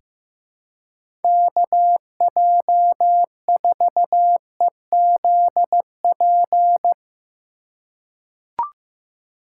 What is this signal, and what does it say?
Morse code, heard as a keyed tone, about 15 wpm, 700 Hz. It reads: KJ4EZP
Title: USA Call Signs - 6 Characters - 2x Character Spacing - 15wpm